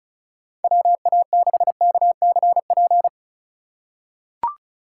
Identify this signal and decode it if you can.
Morse code, heard as a keyed tone, about 34 wpm, 700 Hz. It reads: WA6KCP
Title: USA Call Signs - 6 Characters - 3x Character Spacing - 35wpm